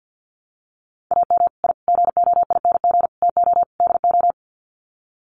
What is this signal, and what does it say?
Morse code, heard as a keyed tone, about 50 wpm, 700 Hz. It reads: AM I GOING TO DO
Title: Sets of 5 Words Encore 50wpm